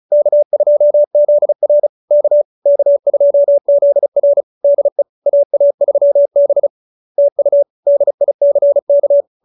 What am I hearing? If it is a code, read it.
Morse code, heard as a keyed tone, about 34 wpm, 590 Hz. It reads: K2ZR K K2ZR DE AA3B TU DICK
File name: QSO302 35